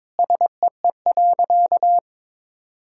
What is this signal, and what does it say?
Morse code, heard as a keyed tone, about 23 wpm, 700 Hz. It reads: SEE.
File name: Sentences with 7-Word Phrases - 22wpm